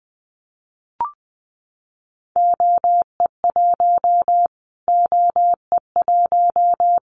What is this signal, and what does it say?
Morse code, heard as a keyed tone, about 20 wpm, 700 Hz. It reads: OE1 OE1
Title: Callsign Prefixes for Top 25 DXCC - Repeated 3x - 20wpm